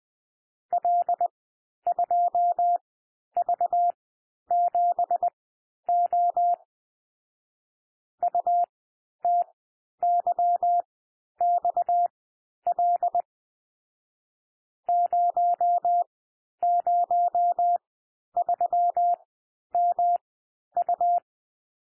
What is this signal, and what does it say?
Morse code, heard as a keyed tone, about 20 wpm, spaced as for 11 wpm, 700 Hz. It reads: L2V7O UTYXL 003MU
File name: cwmp3 / M1 SOUND MIXED 20 wpm